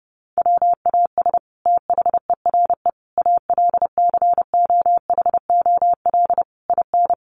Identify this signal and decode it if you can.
Morse code, heard as a keyed tone, about 30 wpm, 700 Hz. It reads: WAS THERE ALCOHOL IN
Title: Sentences from Top 1400 Words 30wpm